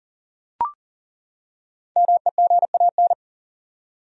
Morse code, heard as a keyed tone, about 40 wpm, 700 Hz. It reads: MEGAN
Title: USA Common Names - 40wpm